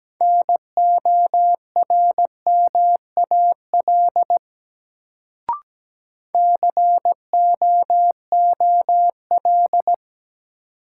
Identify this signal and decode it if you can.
Morse code, heard as a keyed tone, about 17 wpm, 700 Hz. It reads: NORMAL COOL
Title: Sentences from Top 1200 Words - New Word Drill - 17wpm